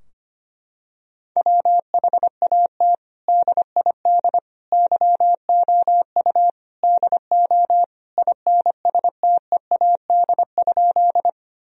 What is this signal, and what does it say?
Morse code, heard as a keyed tone, about 25 wpm, 700 Hz. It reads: WHAT DID YOU DO INSTEAD?
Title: Sentences from Top 1000 Words - Review Part 4 - 25wpm